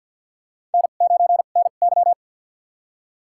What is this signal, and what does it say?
Morse code, heard as a keyed tone, about 53 wpm, spaced as for 37 wpm, 700 Hz. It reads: N9NY
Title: USA Call Signs - 4 Characters - 2x Character Spacing - 50wpm